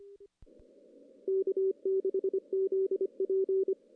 Morse code, heard as a keyed tone, about 24 wpm, 380 Hz. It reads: K6ZP
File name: CW / CW-SS-k5tr- -1700